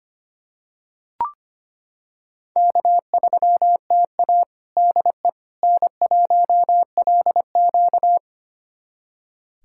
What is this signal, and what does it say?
Morse code, heard as a keyed tone, about 25 wpm, 700 Hz. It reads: K3TA DE N1LQ
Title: USA Callsign DE Callsign - 4 Characters - 25wpm